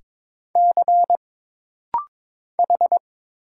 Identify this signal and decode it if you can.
Morse code, heard as a keyed tone, about 22 wpm, 700 Hz. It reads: C H
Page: This rapid-fire timing is geared to push your ICR (Instant Character Recognition) skills.